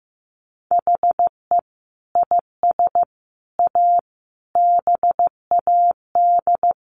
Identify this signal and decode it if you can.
Morse code, heard as a keyed tone, about 15 wpm, 700 Hz. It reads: HE IS A BAD
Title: Sentences from Top 1100 Words 15wpm